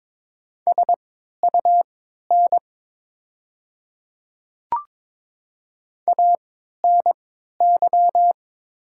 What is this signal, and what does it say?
Morse code, heard as a keyed tone, about 22 wpm, spaced as for 12 wpm, 700 Hz. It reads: SUN ANY
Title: 3x Character Spacing - Top 500 Words - 22wpm